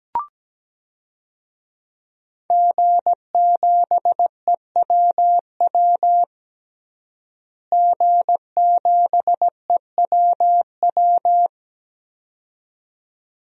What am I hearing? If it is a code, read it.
Morse code, heard as a keyed tone, about 17 wpm, 700 Hz. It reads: G7EWW G7EWW
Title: DXCC-100 Call Signs - 5 Characters - Repeated 2x - 17wpm